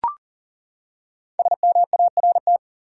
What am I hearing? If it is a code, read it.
Morse code, heard as a keyed tone, about 40 wpm, 700 Hz. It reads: SMART
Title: Five Letter Words - 40wpm